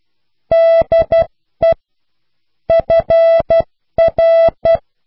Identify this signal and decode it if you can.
Morse code, heard as a keyed tone, about 12 wpm, 660 Hz. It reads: DE FR